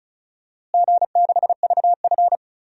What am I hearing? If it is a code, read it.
Morse code, heard as a keyed tone, about 36 wpm, 700 Hz. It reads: G6VF
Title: DXCC-100 Call Signs - 4 Characters - Repeated 3x - 35wpm